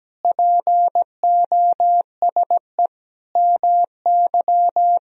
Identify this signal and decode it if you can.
Morse code, heard as a keyed tone, about 17 wpm, 700 Hz. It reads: POSE MY
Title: Sentences from Top 1000 Words - Review Part 2 - 17wpm